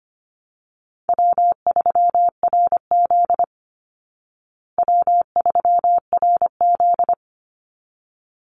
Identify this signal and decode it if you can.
Morse code, heard as a keyed tone, about 25 wpm, 700 Hz. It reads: W3RZ W3RZ
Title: Easy Call Signs Repeated 2x - 25wpm